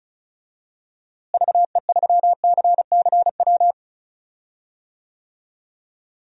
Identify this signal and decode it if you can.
Morse code, heard as a keyed tone, about 36 wpm, 700 Hz. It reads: VE3CCW
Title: DXCC-100 Call Signs - 6 Characters - Repeated 3x - 35wpm